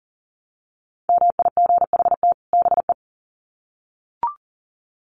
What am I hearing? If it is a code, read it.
Morse code, heard as a keyed tone, about 40 wpm, 700 Hz. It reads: MIGHT BE
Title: Sets of 2 Words 40wpm